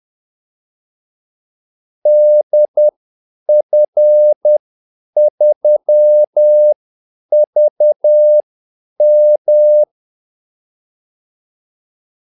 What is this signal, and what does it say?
Morse code, heard as a keyed tone, about 10 wpm, spaced as for 8 wpm, 600 Hz. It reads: DF3VM
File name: Callsigns-10-8wpm